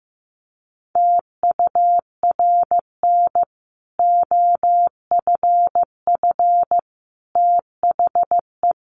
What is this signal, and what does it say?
Morse code, heard as a keyed tone, about 15 wpm, 700 Hz. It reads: TURN OFF THE
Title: Sentences from Top 1400 Words 15wpm